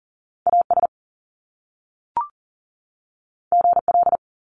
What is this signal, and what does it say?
Morse code, heard as a keyed tone, about 42 wpm, 700 Hz. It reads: AS GL
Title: Top Two Letter Combinations in Words 40wpm